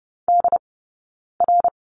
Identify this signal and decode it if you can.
Morse code, heard as a keyed tone, about 30 wpm, spaced as for 9 wpm, 690 Hz. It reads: DR
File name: Lesson 17d - Letter u - Two Characters 30wpm